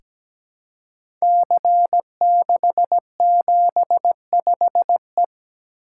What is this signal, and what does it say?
Morse code, heard as a keyed tone, about 17 wpm, 700 Hz. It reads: C675E
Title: Code Groups - 5 Letter:Numbers - 17wpm